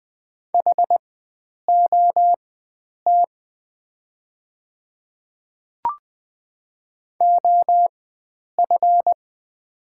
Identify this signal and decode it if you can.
Morse code, heard as a keyed tone, about 20 wpm, spaced as for 9 wpm, 700 Hz. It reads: HOT OF
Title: 4x Character Spacing - Top 100 Words - 20wpm